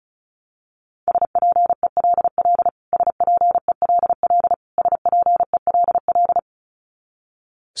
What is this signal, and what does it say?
Morse code, heard as a keyed tone, about 34 wpm, 700 Hz. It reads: SPELL SPELL SPELL
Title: Top 500 Words Repeated 35wpm